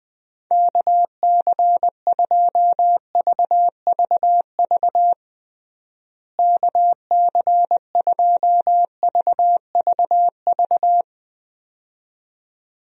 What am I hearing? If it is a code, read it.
Morse code, heard as a keyed tone, about 21 wpm, 700 Hz. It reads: KC2VVV KC2VVV
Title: USA Call Signs - 6 Characters - Repeated 2x - 20wpm